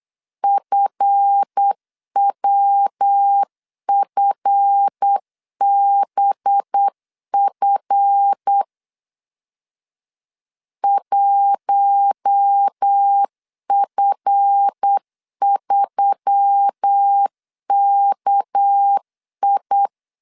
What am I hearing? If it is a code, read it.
Morse code, heard as a keyed tone, about 8 wpm, 800 Hz. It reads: FWFBF 1F3KI
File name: LEZIONI / 7 GF9Z / 7 GF 8wpm 3108